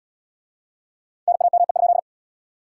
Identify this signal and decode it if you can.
Morse code, heard as a keyed tone, about 79 wpm, 700 Hz. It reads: NAME?